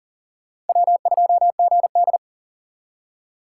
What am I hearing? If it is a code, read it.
Morse code, heard as a keyed tone, about 53 wpm, spaced as for 47 wpm, 700 Hz. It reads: W2GD
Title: Speed Racing - USA Call Signs 40wpm